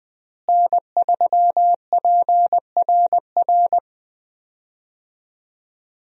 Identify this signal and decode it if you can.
Morse code, heard as a keyed tone, about 20 wpm, 700 Hz. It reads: N3PRR
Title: USA Call Signs - 5 Characters - Repeated 3x - 20wpm